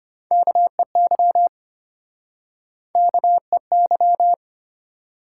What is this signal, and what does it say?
Morse code, heard as a keyed tone, about 32 wpm, 700 Hz. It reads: KEY KEY